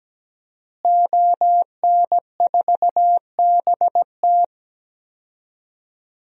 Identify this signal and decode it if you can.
Morse code, heard as a keyed tone, about 17 wpm, 700 Hz. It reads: ON4BT
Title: DXCC-100 Call Signs - 5 Characters - Repeated 2x - 17wpm